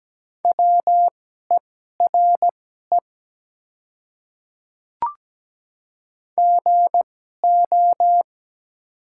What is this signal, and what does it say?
Morse code, heard as a keyed tone, about 17 wpm, spaced as for 12 wpm, 700 Hz. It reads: WERE GO
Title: 2x Character Spacing - Top 100 Words - 17wpm